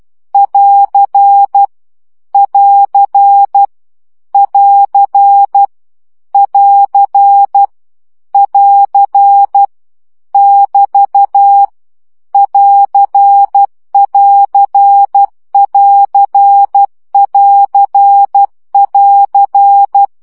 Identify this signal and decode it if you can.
Morse code, heard as a keyed tone, about 12 wpm, 800 Hz. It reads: <AR> <AR> <AR> <AR> <AR> <BT> <AR><AR><AR><AR><AR>
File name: CW gyakorlas plusz 10 baud